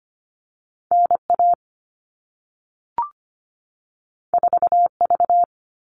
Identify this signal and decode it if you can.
Morse code, heard as a keyed tone, about 25 wpm, 700 Hz. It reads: NA 4V
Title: All Cut Numbers - 2-Digit 25wpm